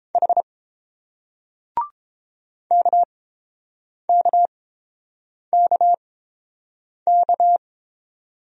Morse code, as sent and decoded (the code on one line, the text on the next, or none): .... / -.- -.- -.- -.-
H KKKK